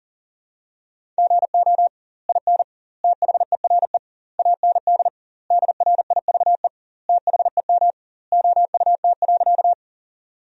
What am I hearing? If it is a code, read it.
Morse code, heard as a keyed tone, about 40 wpm, 700 Hz. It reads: GO IN THERE AND DRIVE THEM OUT.
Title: 2x Word-Spacing - Sentences from Top 500 Words - 40wpm